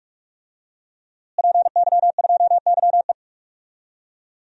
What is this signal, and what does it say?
Morse code, heard as a keyed tone, about 46 wpm, 700 Hz. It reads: PY2YE
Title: DXCC-100 Call Signs - 5 Characters - Repeated 2x - 45wpm